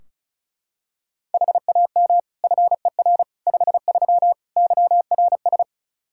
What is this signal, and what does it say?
Morse code, heard as a keyed tone, about 36 wpm, 700 Hz. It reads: HAM FER 53 YRS
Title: Practice QSO - Exchange 3 - More Info - 35wpm